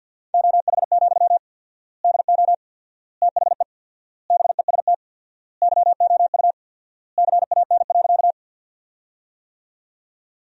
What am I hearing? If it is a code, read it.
Morse code, heard as a keyed tone, about 53 wpm, 700 Hz. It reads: OH, DO THE BEST YOU CAN.
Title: 4x Word-Spacing - Sentences from Top 500 Words - 50wpm